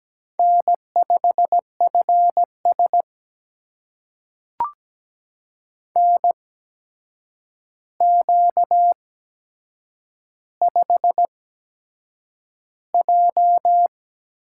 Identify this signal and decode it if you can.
Morse code, heard as a keyed tone, about 17 wpm, spaced as for 7 wpm, 700 Hz. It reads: N5FS NQ5J
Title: USA Call Signs - 4 Characters - 8x Character Spacing - 17wpm